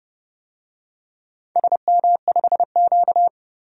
Morse code, heard as a keyed tone, about 28 wpm, 700 Hz. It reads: SM5Q
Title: DXCC-100 Call Signs - 4 Characters - 30wpm